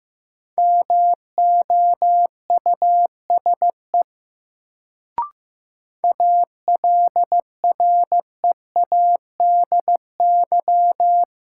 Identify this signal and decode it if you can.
Morse code, heard as a keyed tone, about 15 wpm, 700 Hz. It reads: MOUSE ALREADY
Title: Sentences from Top 700 Words - New Word Drill - 15wpm